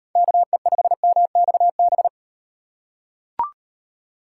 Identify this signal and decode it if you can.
Morse code, heard as a keyed tone, about 37 wpm, 700 Hz. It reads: KE5MXB